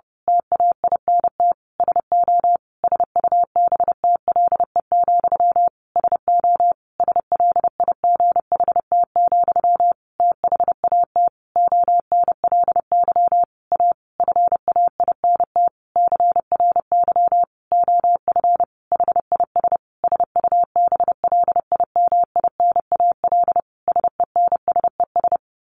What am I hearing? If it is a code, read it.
Morse code, heard as a keyed tone, about 30 wpm, 700 Hz. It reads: TAINT SO SUBTLE, SO SLIGHT, THAT ONLY A FAINT CRY OF HIS SUBLIMINAL SENSES
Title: A Sound Of Thunder 30wpm